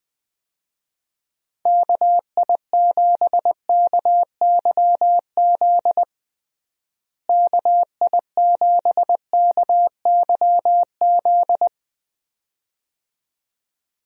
Morse code, heard as a keyed tone, about 20 wpm, 700 Hz. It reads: KI7KYZ KI7KYZ
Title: USA Call Signs - 6 Characters - Repeated 2x - 20wpm